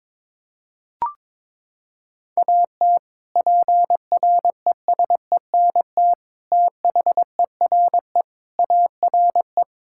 Morse code, heard as a keyed tone, about 22 wpm, 700 Hz. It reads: AT PRESENT THERE ARE
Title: Sentences from Top 1000 Words - Review Part 5 - 22wpm